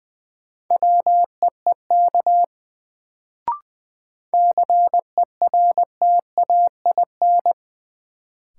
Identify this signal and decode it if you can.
Morse code, heard as a keyed tone, about 20 wpm, 700 Hz. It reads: WEEK CERTAIN
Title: Sentences from Top 500 Words - New Word Drill - 20wpm